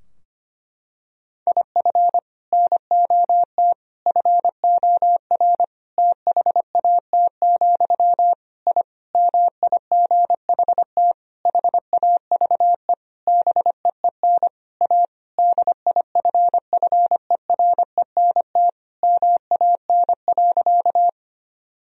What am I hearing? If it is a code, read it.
Morse code, heard as a keyed tone, about 25 wpm, 700 Hz. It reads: IF NOT FOR THAT, I MIGHT HAVE BEEN A DIFFERENT MAN.
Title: Sentences from Top 1000 Words - Review Part 3 - 25wpm